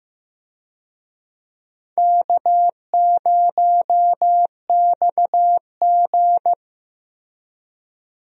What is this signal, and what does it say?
Morse code, heard as a keyed tone, about 15 wpm, 700 Hz. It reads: K0XG
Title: USA Call Signs - 4 Characters - Repeated 3x - 15wpm